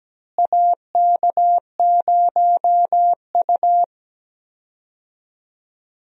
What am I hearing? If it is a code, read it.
Morse code, heard as a keyed tone, about 17 wpm, 700 Hz. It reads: AK0U
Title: USA Call Signs - 4 Characters - Repeated 3x - 17wpm